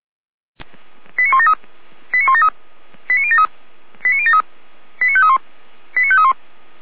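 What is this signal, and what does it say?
Five-tone selective call (ZVEI), heard as quick runs of five tones, tones of about 70 ms. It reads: 89173 89173 89073 89073 89531 89531
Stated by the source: Five-tone ZVEI selective call received by SDR.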